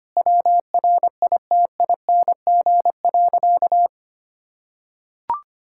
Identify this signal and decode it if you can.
Morse code, heard as a keyed tone, about 25 wpm, 700 Hz. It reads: WRITING.
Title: Sentences from Top 1600 Words 25wpm